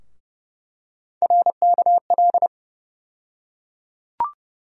Morse code, heard as a keyed tone, about 30 wpm, 700 Hz. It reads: RKL
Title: USA Call Sign Postfixes - 3 Letters 30wpm